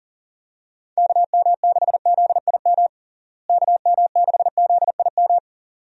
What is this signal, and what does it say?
Morse code, heard as a keyed tone, about 40 wpm, 700 Hz. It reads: KM6ZIM KM6ZIM
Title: USA Call Signs - 6 Characters - Repeated 2x - 40wpm